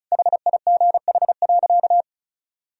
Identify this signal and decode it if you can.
Morse code, heard as a keyed tone, about 36 wpm, 700 Hz. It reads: HIGH.
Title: Sentences from Top 1000 Words - Review Part 2 - 35wpm